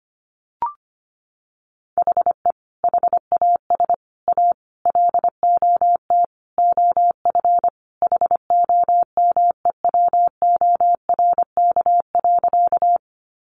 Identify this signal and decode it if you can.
Morse code, heard as a keyed tone, about 25 wpm, 700 Hz. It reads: HE HAS A LOT OF HOMEWORK.
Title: Additional Easy Phrases 25wpm